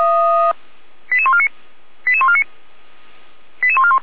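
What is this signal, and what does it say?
Five-tone selective call (ZVEI), heard as quick runs of five tones, tones of about 70 ms. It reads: 88149 88149 88141
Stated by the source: ZVEI five-tone selective call received by SDR.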